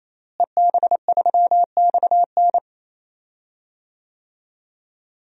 Morse code, heard as a keyed tone, about 28 wpm, 700 Hz. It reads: EB3XN
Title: DXCC-100 Call Signs - 5 Characters - Repeated 3x - 28wpm